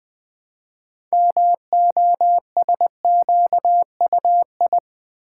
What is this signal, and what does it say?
Morse code, heard as a keyed tone, about 20 wpm, 700 Hz. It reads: MOSQUI
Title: Build A Word - 3 Letters At A Time - 8-Letter Words - 20wpm